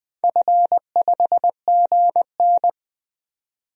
Morse code, heard as a keyed tone, about 20 wpm, 700 Hz. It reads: F5GN
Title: DXCC-100 Call Signs - 4 Characters - 20wpm